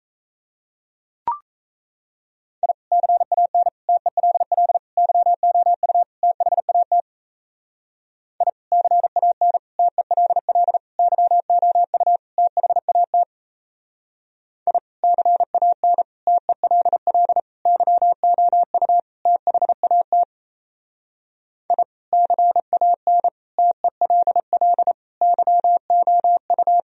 Morse code, heard as a keyed tone, about 35 wpm, 700 Hz. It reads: I CAN TELL YOU THAT I CAN TELL YOU THAT I CAN TELL YOU THAT I CAN TELL YOU